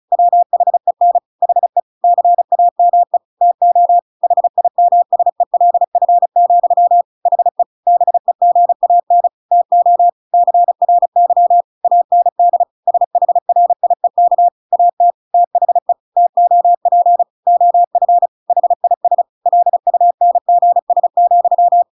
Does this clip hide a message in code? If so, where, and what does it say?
Yes: Morse code from 0.1 to 21.8 s, heard as a keyed tone, about 34 wpm, 700 Hz. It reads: WHEN HE CAME TO HIMSELF, HE BEGAN TO CRY AND SHRIEK AT THE TOP OF HIS LUNGS,